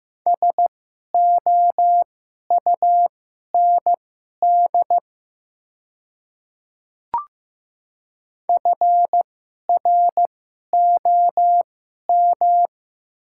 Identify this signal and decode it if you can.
Morse code, heard as a keyed tone, about 15 wpm, spaced as for 11 wpm, 700 Hz. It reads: SOUND FROM
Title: 2x Character Spacing - Top 100 Words - 15wpm